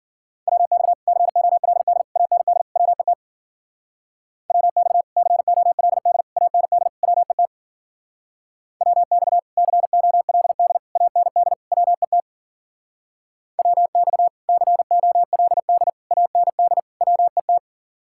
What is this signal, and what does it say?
Morse code, heard as a keyed tone, about 48 wpm, 700 Hz. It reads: WX COLD AND WET WX COLD AND WET WX COLD AND WET WX COLD AND WET
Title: Speed-Racing Common QSO Phrases Encore - 40wpm